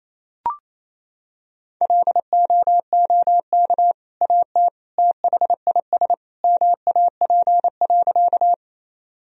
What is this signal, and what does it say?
Morse code, heard as a keyed tone, about 28 wpm, 700 Hz. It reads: LOOK AT THIS MAP.
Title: Sentences from Top 1000 Words - Review Part 2 - 28wpm